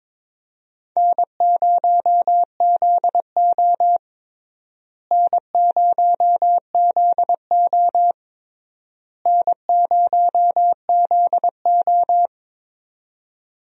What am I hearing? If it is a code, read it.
Morse code, heard as a keyed tone, about 22 wpm, 700 Hz. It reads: N0ZO N0ZO N0ZO
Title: USA Call Signs - 4 Characters - Repeated 3x - 22wpm